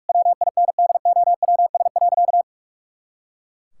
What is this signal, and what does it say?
Morse code, heard as a keyed tone, about 41 wpm, 700 Hz. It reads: WINDOWS.
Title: Pangram Challenge v2 - 45wpm